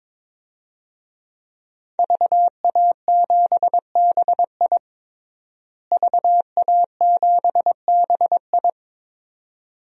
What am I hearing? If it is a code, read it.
Morse code, heard as a keyed tone, about 21 wpm, 700 Hz. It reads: VA7BI VA7BI
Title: DXCC-100 Call Signs - 5 Characters - Repeated 2x - 22wpm